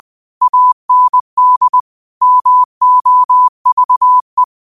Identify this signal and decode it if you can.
Morse code, heard as a keyed tone, about 20 wpm, 1000 Hz. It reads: AND MOVE
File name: qotdfiles / 20wpm